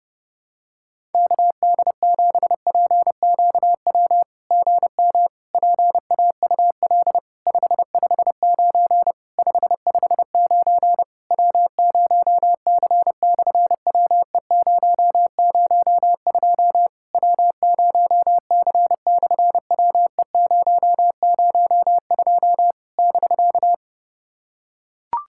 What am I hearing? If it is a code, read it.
Morse code, heard as a keyed tone, about 31 wpm, 700 Hz. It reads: KD7PQW GM PAUL 559 559 W0C/WE002 W0C/WE002 <BK>
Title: SOTA QSO Practice 30wpm